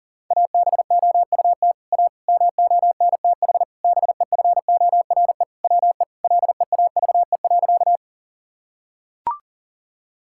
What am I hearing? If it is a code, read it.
Morse code, heard as a keyed tone, about 40 wpm, 700 Hz. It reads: ABOUT A MONTH BEFORE WE LEAVE.
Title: Sentences from Top 1000 Words - Review Part 5 - 40wpm